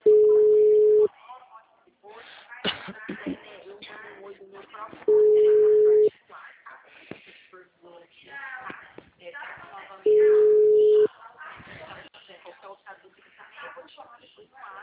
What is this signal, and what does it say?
Ringback tone (Europe, 425 Hz), with speech.